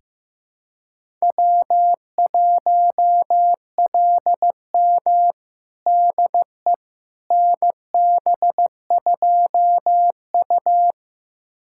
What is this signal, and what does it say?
Morse code, heard as a keyed tone, about 15 wpm, 700 Hz. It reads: W1LM DE NB2U
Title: USA Callsign DE Callsign - 4 Characters - 15wpm